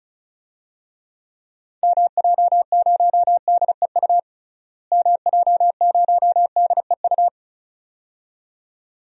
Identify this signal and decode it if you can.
Morse code, heard as a keyed tone, about 34 wpm, 700 Hz. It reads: MJ0DEU MJ0DEU
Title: DXCC-100 Call Signs - 6 Characters - Repeated 2x - 35wpm